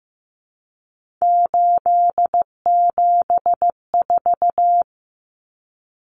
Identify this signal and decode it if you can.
Morse code, heard as a keyed tone, about 15 wpm, 700 Hz. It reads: 874
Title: Numbers - 3-Digit 15wpm